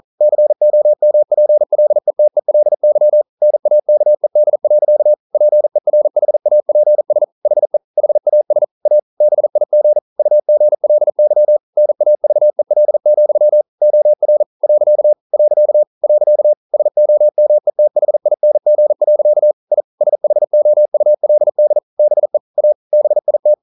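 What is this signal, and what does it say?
Morse code, heard as a keyed tone, about 42 wpm, 600 Hz. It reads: COMPLETELY NAKED. PERHAPS HE HAS A BIG UGLY NAVEL, OR . . SOMETHING. I SHOULD BE A BIT
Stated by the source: Morse code practice at 41 wpm, from CW med Gnister.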